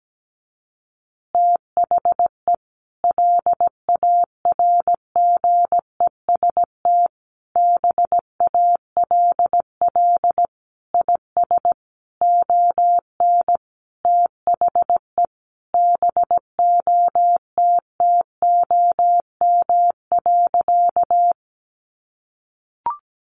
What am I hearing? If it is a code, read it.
Morse code, heard as a keyed tone, about 17 wpm, 700 Hz. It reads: THE LARGEST BALL IS ON THE BOTTOM.
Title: Easy Phrases 17wpm